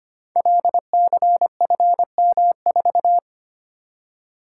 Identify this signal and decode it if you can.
Morse code, heard as a keyed tone, about 25 wpm, 700 Hz. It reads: LCFM4
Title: Code Groups - 5 Letter:Numbers - 25wpm